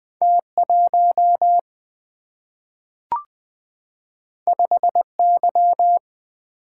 Morse code, heard as a keyed tone, about 20 wpm, 700 Hz. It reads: T1 5Y
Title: Single-Letter Single-Number 20wpm